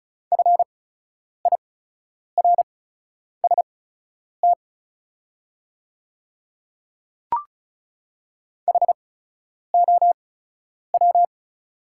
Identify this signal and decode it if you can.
Morse code, heard as a keyed tone, about 36 wpm, spaced as for 10 wpm, 690 Hz. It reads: FIRST HOW
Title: 8x Character Spacing - Top 100 Words - 35wpm